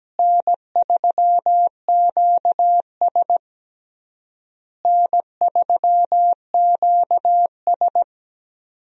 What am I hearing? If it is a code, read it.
Morse code, heard as a keyed tone, about 17 wpm, 700 Hz. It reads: N3QS N3QS
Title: USA Call Signs - 4 Characters - Repeated 3x - 17wpm